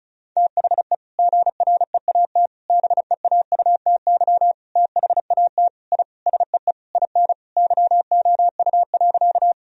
Morse code, heard as a keyed tone, about 34 wpm, 700 Hz. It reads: THE GREAT BEAUTY THAT I SEE IN YOU.
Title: Sentences from Top 1000 Words - Review Part 2 - 35wpm